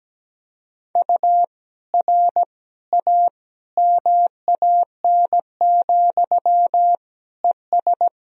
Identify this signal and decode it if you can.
Morse code, heard as a keyed tone, about 17 wpm, 700 Hz. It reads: U R A MAN, ES
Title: Sentences from Top 200 Words and Abbreviations 17wpm